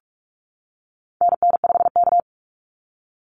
Morse code, heard as a keyed tone, about 44 wpm, 700 Hz. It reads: NN5K
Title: NA SOTA Activator Callsigns 45wpm